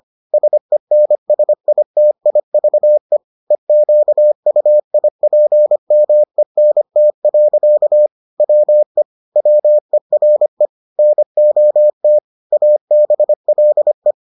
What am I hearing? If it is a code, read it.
Morse code, heard as a keyed tone, about 25 wpm, 600 Hz. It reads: SENSITIVE EQUIPMENT. WE WERE NOT ABLE